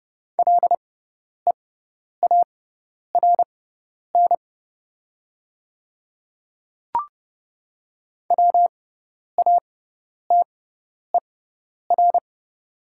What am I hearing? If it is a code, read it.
Morse code, heard as a keyed tone, about 30 wpm, spaced as for 10 wpm, 690 Hz. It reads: LEARN WATER
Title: 6x Character Spacing - Top 300 Words - 30wpm